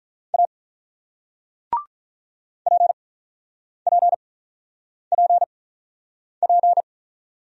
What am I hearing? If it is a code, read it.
Morse code, heard as a keyed tone, about 44 wpm, spaced as for 9 wpm, 700 Hz. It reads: A PPPP
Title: Speed-Racing Single Letter-Number 35wpm